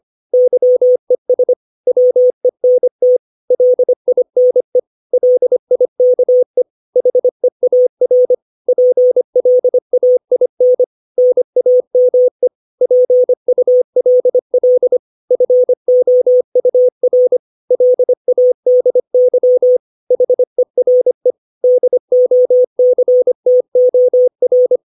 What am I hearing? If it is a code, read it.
Morse code, heard as a keyed tone, about 25 wpm, 500 Hz. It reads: YES WENT LINE LIKE HEAR PLAIN NAME PULL FOUR LADY HERE DOCTOR
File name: 632 words / 632 Words 25WPM